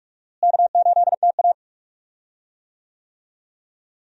Morse code, heard as a keyed tone, about 46 wpm, 700 Hz. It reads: K8TA
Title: USA Call Signs - 4 Characters - Repeated 2x - 45wpm